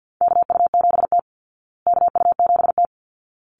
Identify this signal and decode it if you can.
Morse code, heard as a keyed tone, about 50 wpm, 700 Hz. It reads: KU7T KU7T
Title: Call Signs Repeated 3x - 50wpm